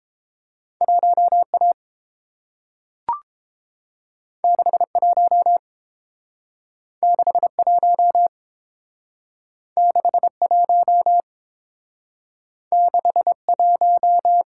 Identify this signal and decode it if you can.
Morse code, heard as a keyed tone, about 30 wpm, 700 Hz. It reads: 1A 61 61 61 61